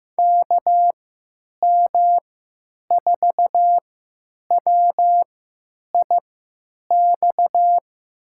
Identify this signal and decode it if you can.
Morse code, heard as a keyed tone, about 15 wpm, spaced as for 8 wpm, 700 Hz. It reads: KM4WIX
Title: USA Call Signs - 6 Characters - 3x Character Spacing - 15wpm